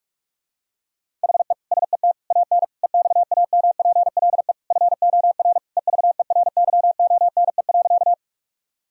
Morse code, heard as a keyed tone, about 41 wpm, 700 Hz. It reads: HE SET AN EXAMPLE FOR EVERYONE.
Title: Sentences from Top 1000 Words - Review Part 1 - 45wpm